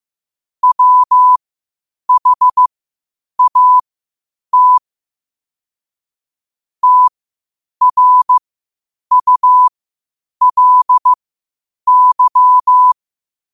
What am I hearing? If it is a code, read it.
Morse code, heard as a keyed tone, about 15 wpm, spaced as for 8 wpm, 1000 Hz. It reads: WHAT TRULY
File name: qotdfiles / 8wpm